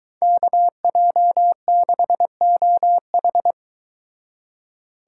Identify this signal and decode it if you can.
Morse code, heard as a keyed tone, about 23 wpm, 700 Hz. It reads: KJ6OH